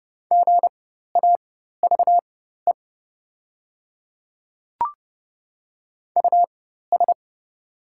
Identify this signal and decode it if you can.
Morse code, heard as a keyed tone, about 30 wpm, spaced as for 14 wpm, 690 Hz. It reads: GAVE US
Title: 4x Character Spacing - Top 500 Words - 30wpm